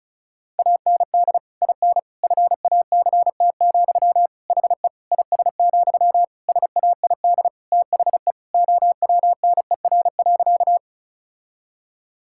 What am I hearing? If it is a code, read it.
Morse code, heard as a keyed tone, about 34 wpm, 700 Hz. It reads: AND IN FACT, HE IS, SAID THE OWNER.
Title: Sentences from Top 1600 Words 35wpm